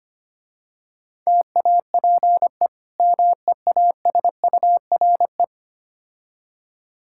Morse code, heard as a keyed tone, about 25 wpm, 700 Hz. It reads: TAPE MEASURE
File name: practice / 25wpm / Difficult Words with Hints - 25wpm